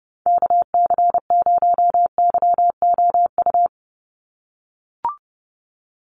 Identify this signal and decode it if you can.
Morse code, heard as a keyed tone, about 30 wpm, 700 Hz. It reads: KC0YOU
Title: USA Call Signs 30wpm